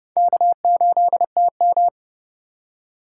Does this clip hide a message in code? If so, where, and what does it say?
Yes: Morse code from 0.2 to 1.9 s, heard as a keyed tone, about 28 wpm, 700 Hz. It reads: K8TM